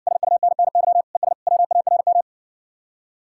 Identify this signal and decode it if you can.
Morse code, heard as a keyed tone, about 59 wpm, 700 Hz. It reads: SUNNY ES WARM